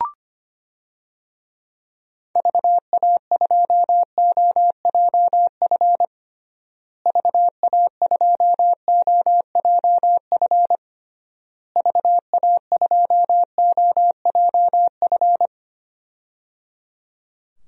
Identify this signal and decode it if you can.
Morse code, heard as a keyed tone, about 25 wpm, 700 Hz. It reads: VA2OJF VA2OJF VA2OJF
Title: DXCC-100 Call Signs - 6 Characters - Repeated 3x - 25wpm